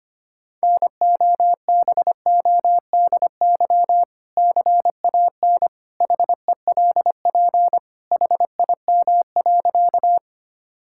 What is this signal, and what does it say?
Morse code, heard as a keyed tone, about 25 wpm, 700 Hz. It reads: NOBODY CAN HELP HIM.
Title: Sentences from Top 1600 Words 25wpm